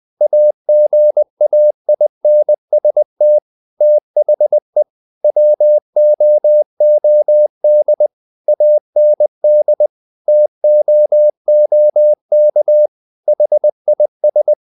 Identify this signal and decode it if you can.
Morse code, heard as a keyed tone, about 20 wpm, 600 Hz. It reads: AGAINST THE WOOD AND TOOK HIS
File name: the old man and the sea / the old man and the sea part 1 20wpm